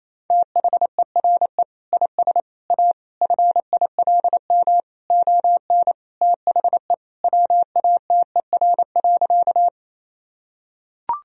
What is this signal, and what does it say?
Morse code, heard as a keyed tone, about 28 wpm, 700 Hz. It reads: THERE IS A FILM ON THE WATER.
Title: Sentences from Top 1000 Words - Review Part 3 - 28wpm